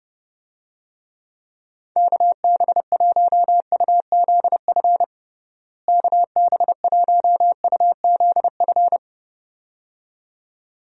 Morse code, heard as a keyed tone, about 30 wpm, 700 Hz. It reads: KB1UZF KB1UZF
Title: USA Call Signs - 6 Characters - Repeated 2x - 30wpm